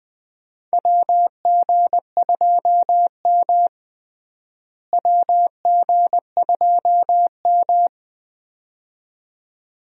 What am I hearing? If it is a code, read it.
Morse code, heard as a keyed tone, about 21 wpm, 700 Hz. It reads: WG2M WG2M
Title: USA Call Signs - 4 Characters - Repeated 2x - 20wpm